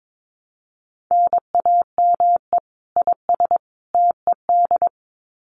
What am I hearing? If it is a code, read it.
Morse code, heard as a keyed tone, about 22 wpm, 700 Hz. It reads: NAME IS TED
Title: Common QSO Phrases 22wpm